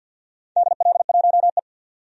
Morse code, heard as a keyed tone, about 52 wpm, 700 Hz. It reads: DL1E
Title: DXCC-100 Call Signs - 4 Characters - 50wpm